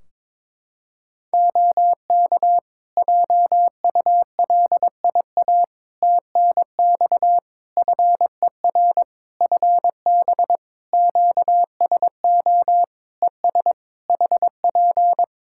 Morse code, heard as a keyed tone, about 22 wpm, 700 Hz. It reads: OK JULIA TNX FER FB QSO ES HP
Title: Practice QSO - Exchange 4 - Ending - 22wpm